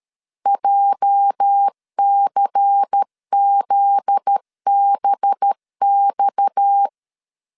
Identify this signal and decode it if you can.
Morse code, heard as a keyed tone, about 13 wpm, 800 Hz. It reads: JCZBX
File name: LEZIONI / 8 YJ45 / 8 YJ 12wpm 2235